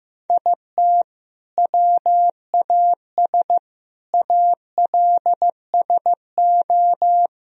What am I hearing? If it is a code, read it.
Morse code, heard as a keyed tone, about 15 wpm, 700 Hz. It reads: IT WAS ALSO
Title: Sentences from Top 1000 Words - Review Part 1 - 15wpm